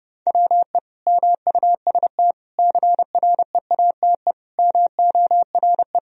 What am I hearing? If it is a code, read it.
Morse code, heard as a keyed tone, about 32 wpm, 700 Hz. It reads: WE MUST CREATE MORE
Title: Sentences from Top 1000 Words - Review Part 3 - 30wpm